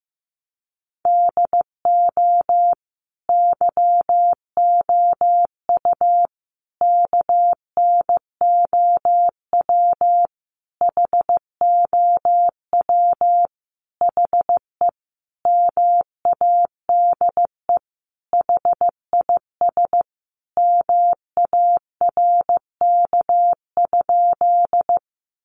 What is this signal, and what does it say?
Morse code, heard as a keyed tone, about 15 wpm, 700 Hz. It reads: DO YOU KNOW HOW HE MADE HIS MARK?
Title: Sentences from Top 300 Words 15wpm